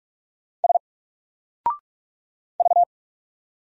Morse code, heard as a keyed tone, about 44 wpm, 700 Hz. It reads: S V
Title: Single Character (Letter:Number:Punctuation:Prosign) 45wpm